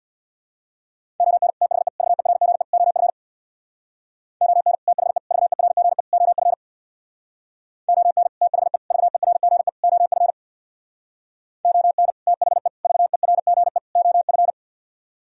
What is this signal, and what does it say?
Morse code, heard as a keyed tone, about 64 wpm, 700 Hz. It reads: ON THE VERGE OF ON THE VERGE OF ON THE VERGE OF ON THE VERGE OF